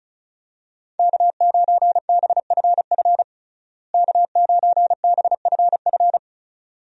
Morse code, heard as a keyed tone, about 34 wpm, 700 Hz. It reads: K9BFF K9BFF
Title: USA Call Signs - 5 Characters - Repeated 2x - 35wpm